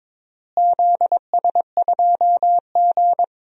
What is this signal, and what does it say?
Morse code, heard as a keyed tone, about 22 wpm, 700 Hz. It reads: ZS2G
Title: DXCC-100 Call Signs - 4 Characters - Repeated 3x - 22wpm